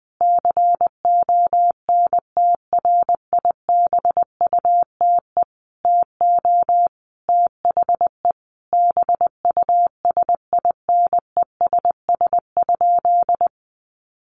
Morse code, heard as a keyed tone, about 20 wpm, 700 Hz. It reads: CONTRIBUTE TO THE BUSINESS?
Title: Sentences from Top 1400 Words 20wpm